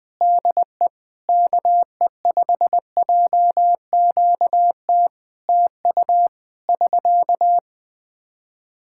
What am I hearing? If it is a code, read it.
Morse code, heard as a keyed tone, about 20 wpm, 700 Hz. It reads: DE KE5JQT TU <SK>
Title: Practice QSO - Exchange 4 - Ending - 20wpm